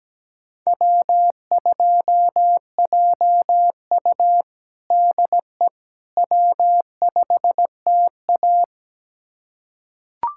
Morse code, heard as a keyed tone, about 17 wpm, 700 Hz. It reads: W2JU DE W5TA
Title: USA Callsign DE Callsign - 4 Characters - 17wpm